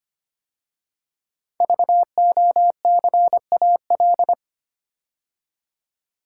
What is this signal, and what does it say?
Morse code, heard as a keyed tone, about 25 wpm, 700 Hz. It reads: VOCAL
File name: 5 Letter Words - First 3 Letters Given - 25wpm